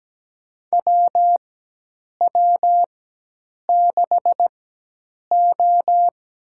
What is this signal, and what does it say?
Morse code, heard as a keyed tone, about 17 wpm, spaced as for 8 wpm, 700 Hz. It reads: WW6O
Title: USA Call Signs - 4 Characters - 4x Character Spacing - 17wpm